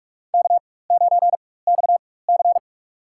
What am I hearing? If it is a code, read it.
Morse code, heard as a keyed tone, about 44 wpm, spaced as for 21 wpm, 700 Hz. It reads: K9XC
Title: USA Call Signs - 4 Characters - 4x Character Spacing - 45wpm